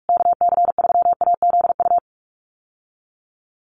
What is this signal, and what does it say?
Morse code, heard as a keyed tone, about 44 wpm, 700 Hz. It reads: KC3AZU
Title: USA Call Signs Repeated 2x 45wpm